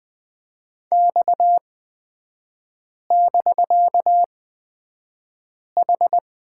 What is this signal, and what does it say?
Morse code, heard as a keyed tone, about 20 wpm, spaced as for 5 wpm, 700 Hz. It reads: X<BK>H
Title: 250 Lesson 40e - BK - Three Characters 20wpm